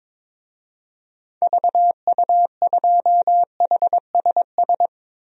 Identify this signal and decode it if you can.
Morse code, heard as a keyed tone, about 22 wpm, 700 Hz. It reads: VU2HSS
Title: DXCC-100 Call Signs - 6 Characters - 22wpm